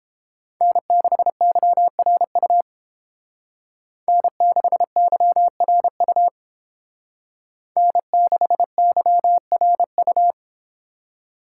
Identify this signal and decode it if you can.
Morse code, heard as a keyed tone, about 30 wpm, 700 Hz. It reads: N6YRU N6YRU N6YRU